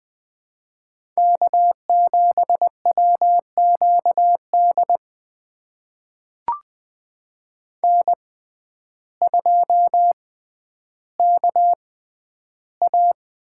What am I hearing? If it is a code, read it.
Morse code, heard as a keyed tone, about 20 wpm, spaced as for 11 wpm, 700 Hz. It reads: K7WQD N2KA
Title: USA Call Signs - 5 Characters - 6x Character Spacing - 20wpm